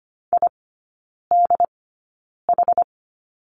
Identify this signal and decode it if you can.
Morse code, heard as a keyed tone, about 25 wpm, spaced as for 9 wpm, 700 Hz. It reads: IDH
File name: Lesson 17e - Letter u - Three Characters 25wpm